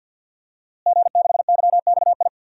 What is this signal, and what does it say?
Morse code, heard as a keyed tone, about 51 wpm, 700 Hz. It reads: G6YXI
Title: DXCC-100 Call Signs - 5 Characters - Repeated 3x - 50wpm